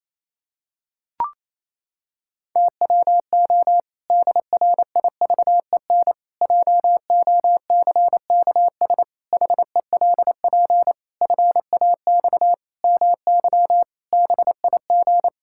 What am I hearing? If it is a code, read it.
Morse code, heard as a keyed tone, about 28 wpm, 700 Hz. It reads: TWO DRIVEN JOCKS HELP FAX MY BIG
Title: Pangram Challenge v2 - 28wpm